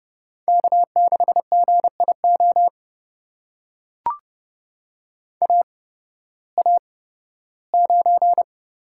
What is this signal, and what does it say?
Morse code, heard as a keyed tone, about 28 wpm, spaced as for 16 wpm, 700 Hz. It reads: K6GIO AA9
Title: USA Call Signs - 5 Characters - 8x Character Spacing - 30wpm